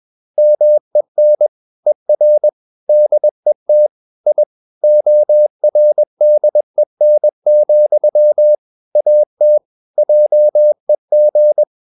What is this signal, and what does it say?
Morse code, heard as a keyed tone, about 21 wpm, 600 Hz. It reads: MEN ER DET I ORDEN, AT JEG
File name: Ferie QRS 21wpm